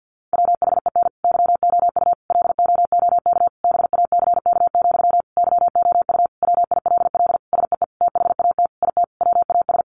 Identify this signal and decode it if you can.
Morse code, heard as a keyed tone, about 50 wpm, 700 Hz. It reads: WHEN YOU LOOK BACK, YOU WILL SEE THAT IT WAS
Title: Sentences from Top 800 Words 50wpm